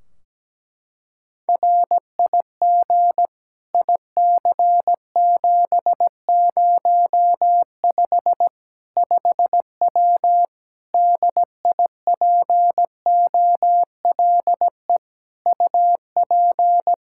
Morse code, heard as a keyed tone, about 17 wpm, 700 Hz. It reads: RIG IC705 5W DIPOLE UP
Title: K1USN SST QSO Practice 17wpm